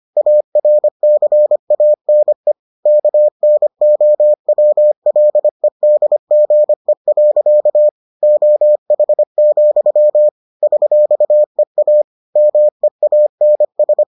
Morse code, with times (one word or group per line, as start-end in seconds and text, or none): 0.2-2.5 ARCANE
2.9-7.9 KNOWLEDGE.
8.2-10.3 OH,
10.6-12.0 $EA
12.4-14.0 MEANS